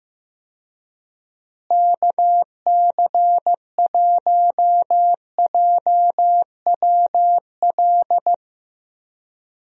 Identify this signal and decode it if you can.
Morse code, heard as a keyed tone, about 15 wpm, 700 Hz. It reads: KC1JWL
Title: USA Call Signs - 6 Characters - Repeated 2x - 15wpm